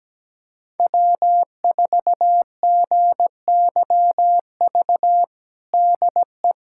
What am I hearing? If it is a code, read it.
Morse code, heard as a keyed tone, about 17 wpm, 700 Hz. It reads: W4GYV DE
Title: USA Callsign DE Callsign - 5 Characters - 17wpm